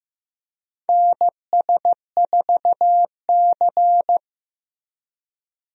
Morse code, heard as a keyed tone, about 15 wpm, 700 Hz. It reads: NS4C
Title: USA Call Signs - 4 Characters - 15wpm